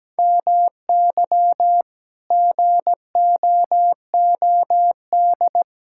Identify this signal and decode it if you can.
Morse code, heard as a keyed tone, about 17 wpm, 700 Hz. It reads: MY GOOD
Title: Sentences from Top 1600 Words 17wpm